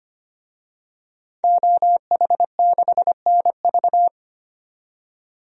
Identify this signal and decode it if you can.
Morse code, heard as a keyed tone, about 25 wpm, 700 Hz. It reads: OH6NV
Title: DXCC-100 Call Signs - 5 Characters - 25wpm